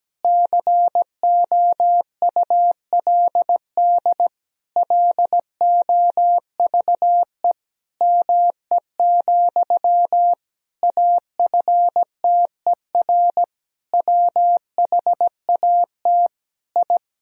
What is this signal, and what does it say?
Morse code, heard as a keyed tone, about 17 wpm, 700 Hz. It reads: COULD LOVE ME, AFTER WHAT I
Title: Sentences from Top 1000 Words - Review Part 1 - 17wpm